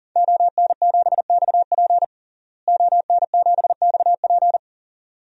Morse code, heard as a keyed tone, about 40 wpm, 700 Hz. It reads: ON7XP ON7XP
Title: DXCC-100 Call Signs - 5 Characters - Repeated 2x - 40wpm